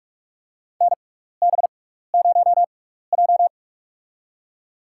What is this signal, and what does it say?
Morse code, heard as a keyed tone, about 46 wpm, spaced as for 15 wpm, 700 Hz. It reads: NB0J
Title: USA Call Signs - 4 Characters - 6x Character Spacing - 45wpm